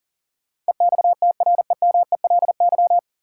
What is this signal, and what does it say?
Morse code, heard as a keyed tone, about 37 wpm, 700 Hz. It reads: EXTREMELY
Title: Sentences from Top 1200 Words - New Word Drill Alt - 40wpm